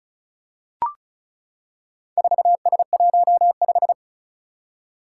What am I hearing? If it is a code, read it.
Morse code, heard as a keyed tone, about 36 wpm, 700 Hz. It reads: 4S15
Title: Four Challenging Letter-Number Combinations 35wpm